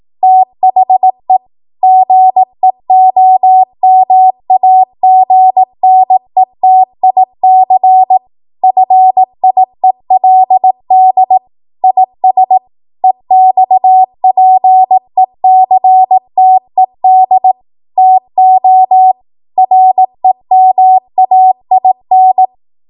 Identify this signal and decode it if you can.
Morse code, heard as a keyed tone, about 18 wpm, 750 Hz. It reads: THE GEOMAGNETIC FIELD IS EXPECTED TO REMAIN
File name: Morse / 18WPM